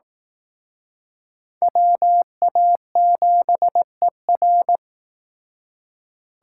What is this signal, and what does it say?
Morse code, heard as a keyed tone, about 18 wpm, 700 Hz. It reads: WA7ER
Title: Speed Racing - USA Call Signs 15wpm